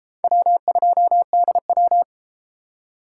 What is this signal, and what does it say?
Morse code, heard as a keyed tone, about 30 wpm, 700 Hz. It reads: W2DW